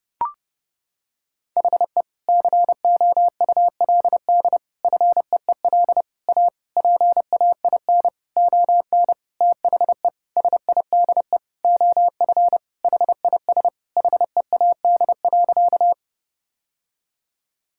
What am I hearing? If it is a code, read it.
Morse code, heard as a keyed tone, about 28 wpm, 700 Hz. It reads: HE COULD FEEL A PAIN ON THE SIDE OF HIS HEAD.
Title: Sentences from Top 1600 Words 30wpm